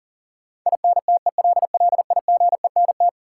Morse code, heard as a keyed tone, about 40 wpm, 700 Hz. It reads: INTELLIGENT
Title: Sentences from Top 1000 Words - Review Part 7 - New Word Drill - 40wpm